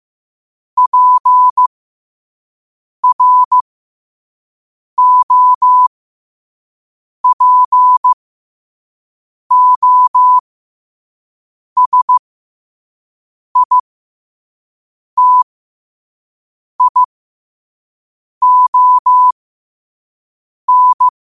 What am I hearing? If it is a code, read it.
Morse code, heard as a keyed tone, about 15 wpm, spaced as for 5 wpm, 1000 Hz. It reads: PROPOSITION
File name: qotdfiles / 5wpm